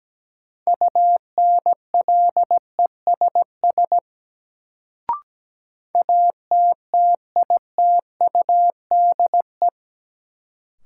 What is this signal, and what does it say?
Morse code, heard as a keyed tone, about 17 wpm, 700 Hz. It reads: UNLESS ATTITUDE
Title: Sentences from Top 1300 Words - New Word Drill Alt - 17wpm